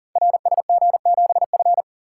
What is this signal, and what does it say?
Morse code, heard as a keyed tone, about 38 wpm, 700 Hz. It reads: RSG7F
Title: Code Groups - 5 Letter:Numbers - 40wpm